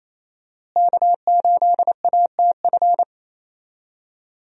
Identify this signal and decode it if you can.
Morse code, heard as a keyed tone, about 28 wpm, 700 Hz. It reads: K8ATF
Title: USA Call Signs - 5 Characters - Repeated 2x - 28wpm